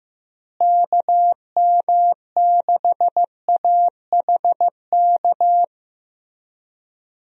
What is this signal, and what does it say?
Morse code, heard as a keyed tone, about 15 wpm, 700 Hz. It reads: KM6AHK
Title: USA Call Signs - 6 Characters - Repeated 3x - 15wpm